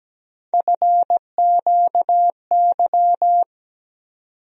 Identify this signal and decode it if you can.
Morse code, heard as a keyed tone, about 17 wpm, 700 Hz. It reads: FQY
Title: USA Call Sign Postfixes - 3 Letters 17wpm